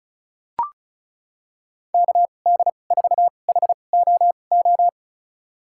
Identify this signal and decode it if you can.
Morse code, heard as a keyed tone, about 34 wpm, spaced as for 25 wpm, 700 Hz. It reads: KD4HOO
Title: USA Call Signs - 2x Character Spacing - 35wpm